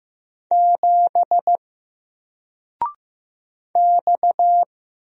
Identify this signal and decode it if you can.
Morse code, heard as a keyed tone, about 15 wpm, 700 Hz. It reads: 7 X
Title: Single Character (Letter:Number:Punctuation:Prosign) Alt 15wpm